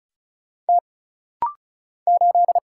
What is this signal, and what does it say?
Morse code, heard as a keyed tone, about 36 wpm, 700 Hz. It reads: T 8
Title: Single Letter-Number Alt - Rapid-Fire 35wpm